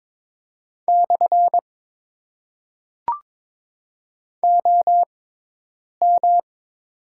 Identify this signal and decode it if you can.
Morse code, heard as a keyed tone, about 22 wpm, 700 Hz. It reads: / O M
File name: Top 90 QSO Elements - 6x Character Spacing - 22wpm